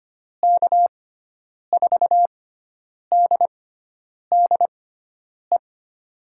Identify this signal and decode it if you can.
Morse code, heard as a keyed tone, about 25 wpm, spaced as for 9 wpm, 700 Hz. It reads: K4DDE
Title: USA Call Signs - 6x Character Spacing - 25wpm